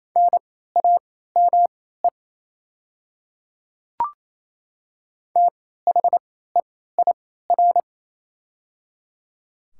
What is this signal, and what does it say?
Morse code, heard as a keyed tone, about 28 wpm, spaced as for 16 wpm, 700 Hz. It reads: NAME THEIR
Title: 3x Character Spacing - Top 200 Words - 28wpm